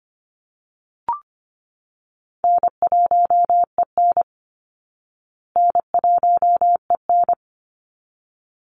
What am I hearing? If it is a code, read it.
Morse code, heard as a keyed tone, about 25 wpm, 700 Hz. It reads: N1EN N1EN
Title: Easy Call Signs Repeated 2x - 25wpm